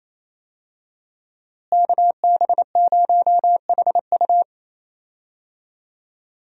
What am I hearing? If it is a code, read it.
Morse code, heard as a keyed tone, about 28 wpm, 700 Hz. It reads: KB0HU
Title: USA Call Signs - 5 Characters - Repeated 2x - 28wpm